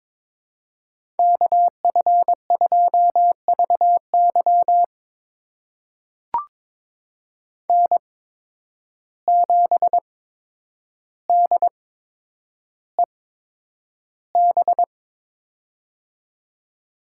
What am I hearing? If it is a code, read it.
Morse code, heard as a keyed tone, about 22 wpm, spaced as for 9 wpm, 700 Hz. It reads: KF2VY N7DEB
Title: USA Call Signs - 5 Characters - 8x Character Spacing - 22wpm